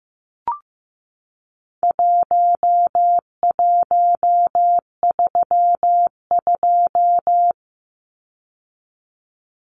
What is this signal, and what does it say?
Morse code, heard as a keyed tone, about 15 wpm, 700 Hz. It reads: 1132
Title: Numbers - Contest Style - 4-Digit 15wpm